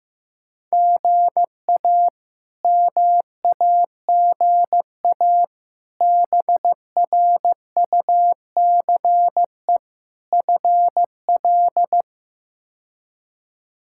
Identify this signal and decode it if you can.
Morse code, heard as a keyed tone, about 16 wpm, 700 Hz. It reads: GA MAGA BRUCE FL
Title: K1USN SST QSO Practice - 15wpm